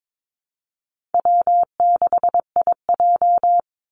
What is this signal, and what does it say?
Morse code, heard as a keyed tone, about 22 wpm, 700 Hz. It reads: W6IJ
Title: Easy Call Signs Repeated 2x - 22wpm